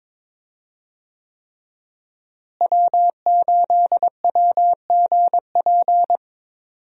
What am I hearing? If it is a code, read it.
Morse code, heard as a keyed tone, about 22 wpm, 700 Hz. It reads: W8WGP
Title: USA Call Signs - 5 Characters - Repeated 3x - 22wpm